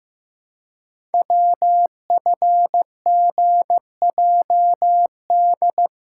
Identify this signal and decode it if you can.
Morse code, heard as a keyed tone, about 15 wpm, 700 Hz. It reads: WFGJD
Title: Code Groups - 5 Letter:Numbers - 15wpm